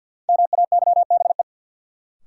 Morse code, heard as a keyed tone, about 46 wpm, 700 Hz. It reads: MAYBE
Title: Sentences from Top 1000 Words - New Word Drill - 50wpm